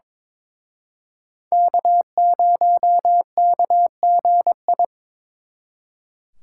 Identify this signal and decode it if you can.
Morse code, heard as a keyed tone, about 26 wpm, 700 Hz. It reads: K0KGI
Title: Speed Racing - USA 1x3 Call Signs 22wpm